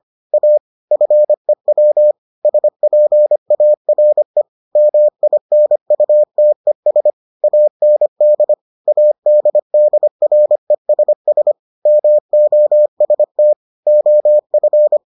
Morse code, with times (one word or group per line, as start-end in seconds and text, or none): 0.3-0.6 A
0.9-2.1 FEW
2.5-4.4 SPARE
4.8-7.1 MINUTES
7.4-8.5 AND
8.9-11.5 ADDRESS
11.9-13.5 MOST
13.9-15.0 OF